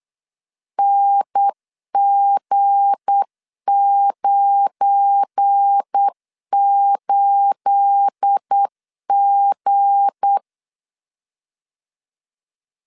Morse code, heard as a keyed tone, about 8 wpm, 800 Hz. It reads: NG98G
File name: LEZIONI / 9 28 8wpm 3512